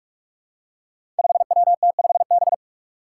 Morse code, heard as a keyed tone, about 42 wpm, 700 Hz. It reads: 5WT5B
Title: Five Challenging Letter-Number Combinations 45wpm